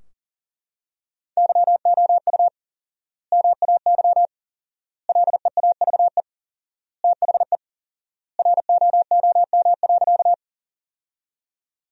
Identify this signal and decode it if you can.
Morse code, heard as a keyed tone, about 40 wpm, 700 Hz. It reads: YOU MAY LEAVE THE ROOM.
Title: 4x Word-Spacing - Sentences from Top 400 Words - 40wpm